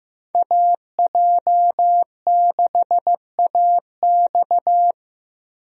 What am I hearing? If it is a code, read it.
Morse code, heard as a keyed tone, about 15 wpm, 700 Hz. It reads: AJ6AX
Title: USA Call Signs - 5 Characters - Repeated 3x - 15wpm